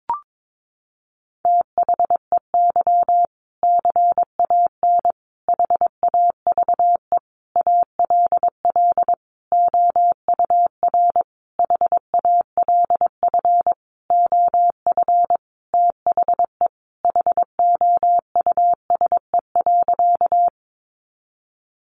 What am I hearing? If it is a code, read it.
Morse code, heard as a keyed tone, about 22 wpm, 700 Hz. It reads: THEY CAN HAVE ALL OUR HALF OF THE HOUSE.
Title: Sentences from Top 400 Words 22wpm